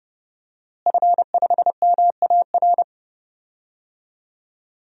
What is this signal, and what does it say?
Morse code, heard as a keyed tone, about 30 wpm, 700 Hz. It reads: F5MAR
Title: DXCC-100 Call Signs - 5 Characters - Repeated 2x - 30wpm